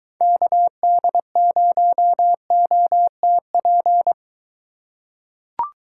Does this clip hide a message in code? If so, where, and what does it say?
Yes: Morse code from 0.2 to 4.1 s, heard as a keyed tone, about 23 wpm, 700 Hz. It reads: KD0OTP